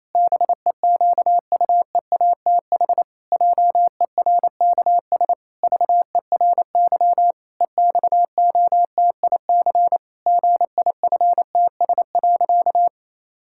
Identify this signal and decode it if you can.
Morse code, heard as a keyed tone, about 28 wpm, 700 Hz. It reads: BEQUEATH JERKS VERY EXOTIC GIFTS.
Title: Pangram Challenge v2 - 28wpm